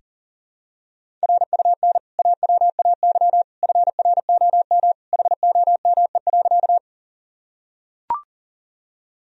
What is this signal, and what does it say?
Morse code, heard as a keyed tone, about 40 wpm, 700 Hz. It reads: RUN AWAY FROM HOME.
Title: Sentences from Top 1000 Words - Review Part 1 - 40wpm